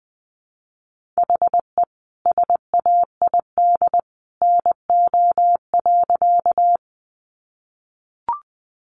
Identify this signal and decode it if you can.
Morse code, heard as a keyed tone, about 20 wpm, 700 Hz. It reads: HE SAID NO.
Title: Sentences from Top 100 Words 20wpm